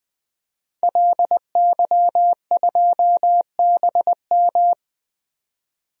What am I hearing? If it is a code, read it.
Morse code, heard as a keyed tone, about 20 wpm, 700 Hz. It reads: LY2BM
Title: DXCC-100 Call Signs - 5 Characters - 20wpm